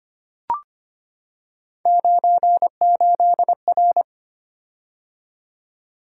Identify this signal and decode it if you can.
Morse code, heard as a keyed tone, about 25 wpm, 700 Hz. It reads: 98R
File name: Three Letter-Number Combinations - 25wpm